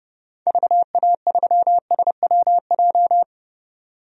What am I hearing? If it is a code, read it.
Morse code, heard as a keyed tone, about 30 wpm, 700 Hz. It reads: VA3SWJ
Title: DXCC-100 Call Signs - 6 Characters - 30wpm